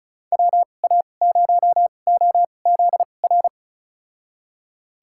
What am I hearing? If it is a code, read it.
Morse code, heard as a keyed tone, about 36 wpm, spaced as for 25 wpm, 700 Hz. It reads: WA0OZR
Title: USA Call Signs - 6 Characters - 2x Character Spacing - 35wpm